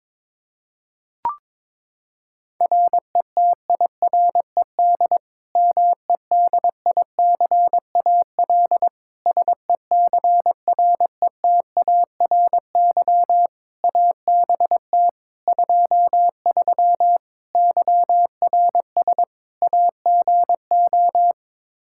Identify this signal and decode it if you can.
Morse code, heard as a keyed tone, about 22 wpm, 700 Hz. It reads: RETIRED MEDICAL SECRETARY ABT 23 YRS AGO
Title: K1USN SST QSO Practice 22wpm